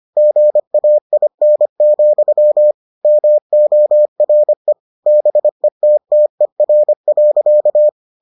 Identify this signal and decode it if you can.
Morse code, heard as a keyed tone, about 25 wpm, 600 Hz. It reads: GAIN, MORE BETTER.